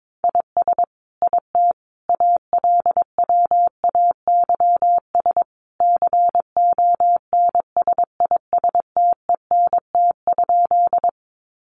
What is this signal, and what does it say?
Morse code, heard as a keyed tone, about 22 wpm, 700 Hz. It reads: IS IT ALWAYS CONSISTENT?
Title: Sentences from Top 1500 Words 22wpm